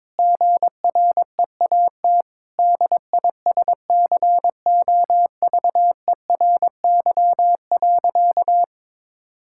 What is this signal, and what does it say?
Morse code, heard as a keyed tone, about 22 wpm, 700 Hz. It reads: GREAT DISCOVERY.
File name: practice / 22wpm / Sentences with 8-Word Phrases - 22wpm